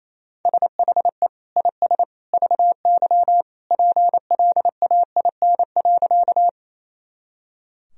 Morse code, heard as a keyed tone, about 28 wpm, 700 Hz. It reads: SHE IS VY PLAIN.
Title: Sentences from Top 400 Words and Abbreviations 28wpm